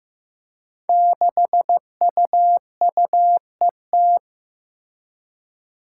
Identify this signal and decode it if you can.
Morse code, heard as a keyed tone, about 15 wpm, 700 Hz. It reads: 6UUET
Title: Five Challenging Letter-Number Combinations 15wpm